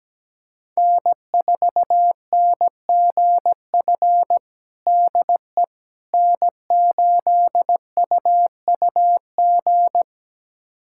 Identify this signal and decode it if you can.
Morse code, heard as a keyed tone, about 17 wpm, 700 Hz. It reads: N4NGF DE N8UUG
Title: USA Callsign DE Callsign - 5 Characters - 17wpm